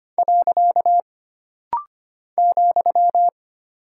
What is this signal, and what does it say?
Morse code, heard as a keyed tone, about 25 wpm, 700 Hz. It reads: . ,
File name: Single Punctuation-Prosign - Rapid-Fire - 25wpm